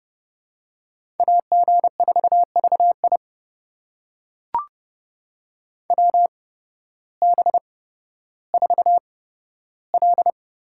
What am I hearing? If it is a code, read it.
Morse code, heard as a keyed tone, about 29 wpm, spaced as for 14 wpm, 700 Hz. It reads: AG4VI WB4L
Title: USA Call Signs - 8x Character Spacing - 30wpm